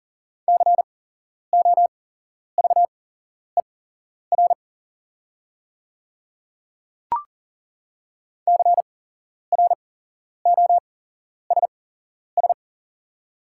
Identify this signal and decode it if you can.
Morse code, heard as a keyed tone, about 40 wpm, spaced as for 11 wpm, 690 Hz. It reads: COVER CROSS
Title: 8x Character Spacing - Top 300 Words - 40wpm